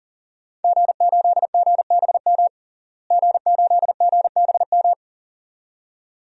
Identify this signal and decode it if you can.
Morse code, heard as a keyed tone, about 40 wpm, 700 Hz. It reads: G8GBM G8GBM
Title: DXCC-100 Call Signs - 5 Characters - Repeated 2x - 40wpm